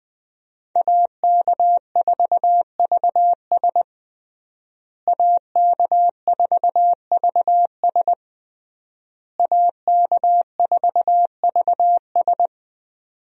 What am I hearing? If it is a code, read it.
Morse code, heard as a keyed tone, about 21 wpm, 700 Hz. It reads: AK4VS AK4VS AK4VS
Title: USA Call Signs - 5 Characters - Repeated 3x - 20wpm